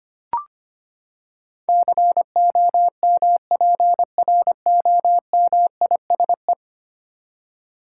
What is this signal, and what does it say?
Morse code, heard as a keyed tone, about 25 wpm, 700 Hz. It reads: COMPROMISE
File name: Ten Letter Words - 25wpm